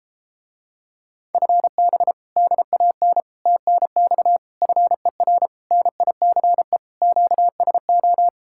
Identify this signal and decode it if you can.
Morse code, heard as a keyed tone, about 33 wpm, 700 Hz. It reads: FB DAN TNX FER NICE QSO